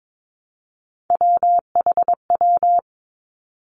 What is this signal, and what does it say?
Morse code, heard as a keyed tone, about 22 wpm, 700 Hz. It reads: WHW
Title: Three Challenging Letter-Number Combinations 22wpm